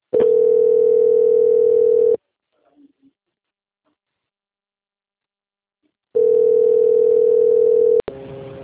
Ringback tone (North America), 440 + 480 Hz.